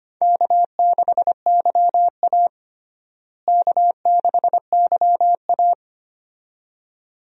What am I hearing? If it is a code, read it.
Morse code, heard as a keyed tone, about 25 wpm, 700 Hz. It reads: K6YA K6YA
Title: USA Call Signs - 4 Characters - Repeated 2x - 25wpm